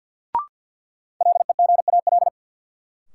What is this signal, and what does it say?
Morse code, heard as a keyed tone, about 53 wpm, 700 Hz. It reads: LEGAL
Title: Sentences from Top 1200 Words - New Word Drill - 50wpm